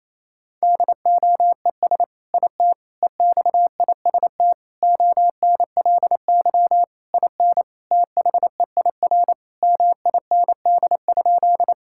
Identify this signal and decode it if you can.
Morse code, heard as a keyed tone, about 28 wpm, 700 Hz. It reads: DOES IT EXIST ONLY IN THEIR MIND?
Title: Sentences from Top 1000 Words - Review Part 5 - 28wpm